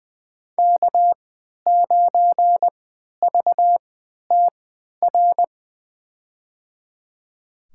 Morse code, heard as a keyed tone, about 21 wpm, spaced as for 11 wpm, 700 Hz. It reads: K9VTR
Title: USA Call Signs - 5 Characters - 3x Character Spacing - 20wpm